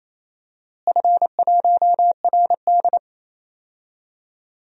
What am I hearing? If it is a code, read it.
Morse code, heard as a keyed tone, about 27 wpm, 700 Hz. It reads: F1RD
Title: DXCC-100 Call Signs - 4 Characters - Repeated 3x - 28wpm